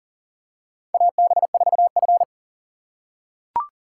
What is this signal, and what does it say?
Morse code, heard as a keyed tone, about 40 wpm, 700 Hz. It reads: AB4F
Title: USA Call Signs - 6x Character Spacing - 40wpm